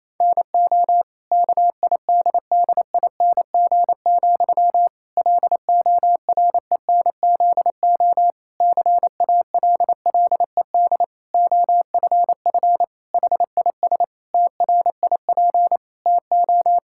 Morse code, heard as a keyed tone, about 27 wpm, 700 Hz. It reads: NO KIDDING, LORENZO CALLED OFF HIS TRIP TO
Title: Pangram Challenge v2 - 28wpm